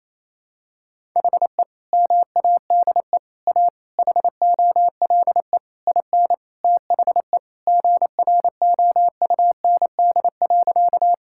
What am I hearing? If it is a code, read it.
Morse code, heard as a keyed tone, about 28 wpm, 700 Hz. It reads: HE MADE A HOLE IN THE GROUND.
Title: Sentences from Top 1600 Words 28wpm